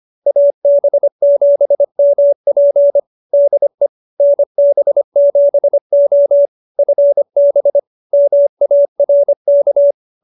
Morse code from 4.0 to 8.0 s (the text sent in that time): NB7O FB